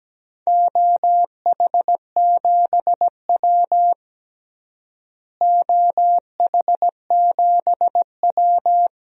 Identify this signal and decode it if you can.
Morse code, heard as a keyed tone, about 17 wpm, 700 Hz. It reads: OH7W OH7W
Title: DXCC-100 Call Signs - 4 Characters - Repeated 2x - 17wpm